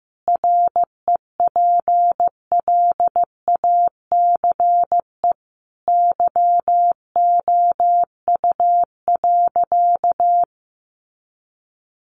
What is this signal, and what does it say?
Morse code, heard as a keyed tone, about 14 wpm, 700 Hz. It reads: REPLACE YOU.
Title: Sentences from Top 1400 Words 15wpm